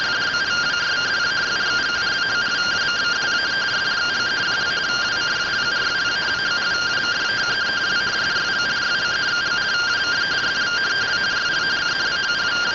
Two-tone teleprinter signal, about 170 Hz shift.
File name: findings / AXIAL fr130 3mm 409.gen-40 rtty rx